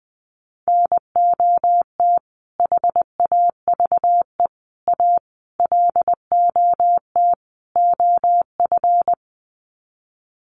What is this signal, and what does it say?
Morse code, heard as a keyed tone, about 20 wpm, 700 Hz. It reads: NOT HAVE A LOT OF
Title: Sets of 5 Words 20wpm